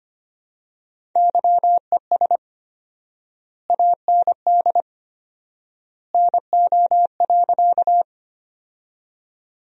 Morse code, heard as a keyed tone, about 25 wpm, 700 Hz. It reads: YES AND NO.
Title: 4x Word-Spacing - Sentences from Top 500 Words - 25wpm